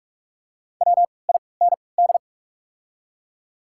Morse code, heard as a keyed tone, about 44 wpm, spaced as for 25 wpm, 700 Hz. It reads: WIND
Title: 3x Character Spacing - Top 500 Words - 45wpm